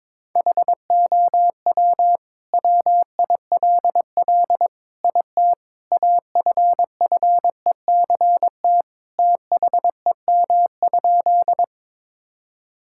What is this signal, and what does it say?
Morse code, heard as a keyed tone, about 22 wpm, 700 Hz. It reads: HOW WILL IT AFFECT THEM?
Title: Sentences from Top 1000 Words - Review Part 6 - 22wpm